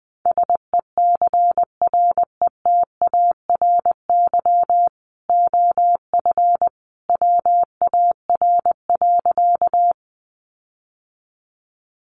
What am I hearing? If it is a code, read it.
Morse code, heard as a keyed tone, about 20 wpm, 700 Hz. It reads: SECRETARY OF WAR.
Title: Sentences from Top 1300 Words 20wpm